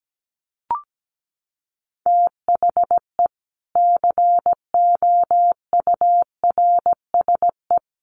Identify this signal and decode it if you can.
Morse code, heard as a keyed tone, about 17 wpm, 700 Hz. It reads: THE COURSE
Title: Sets of 3 Words 17wpm